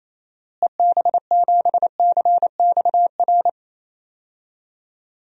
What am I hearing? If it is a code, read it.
Morse code, heard as a keyed tone, about 28 wpm, 700 Hz. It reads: EB7CXR
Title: DXCC-100 Call Signs - 6 Characters - Repeated 3x - 28wpm